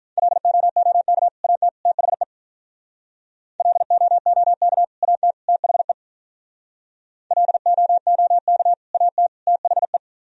Morse code, heard as a keyed tone, about 46 wpm, 700 Hz. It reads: LOOK AT THE LOOK AT THE LOOK AT THE